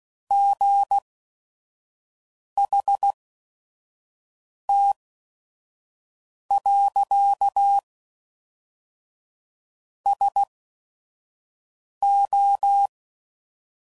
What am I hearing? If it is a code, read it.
Morse code, heard as a keyed tone, about 16 wpm, spaced as for 5 wpm, 800 Hz. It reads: GHT. SO
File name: CW Code Groups @5WPM / QSOs / QSO WA7QOX 1